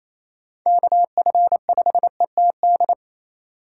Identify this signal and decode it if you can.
Morse code, heard as a keyed tone, about 28 wpm, 700 Hz. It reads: KF5ETD
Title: K1USN SST QSO Practice - 28wpm